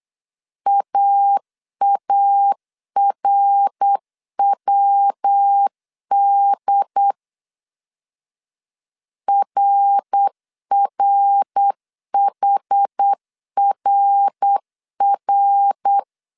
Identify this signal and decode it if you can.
Morse code, heard as a keyed tone, about 8 wpm, 800 Hz. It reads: AARWD RRHRR